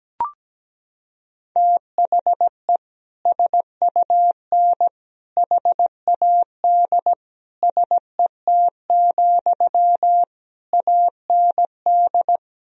Morse code, heard as a keyed tone, about 17 wpm, 700 Hz. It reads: THE SUN HAD SET, AND
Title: Sentences from Top 1000 Words - Review Part 1 - 17wpm